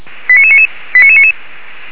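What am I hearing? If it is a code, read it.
Five-tone selective call (ZVEI), heard as quick runs of five tones, tones of about 70 ms. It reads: 89999 89999
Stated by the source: ZVEI five-tone call received on SDR.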